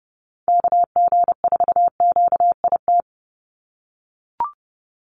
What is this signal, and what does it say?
Morse code, heard as a keyed tone, about 30 wpm, 700 Hz. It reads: KG4QIT
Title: USA Call Signs 30wpm